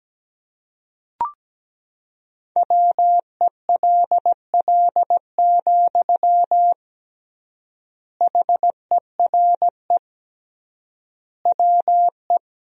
Morse code, heard as a keyed tone, about 17 wpm, 700 Hz. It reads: WELL, HERE WE
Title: 3x Word-Spacing - Sentences from Top 300 Words - 17wpm